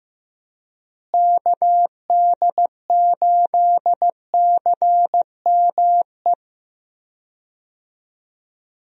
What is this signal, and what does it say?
Morse code, heard as a keyed tone, about 15 wpm, 700 Hz. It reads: KD8CME
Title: USA Call Signs - 6 Characters - Repeated 3x - 15wpm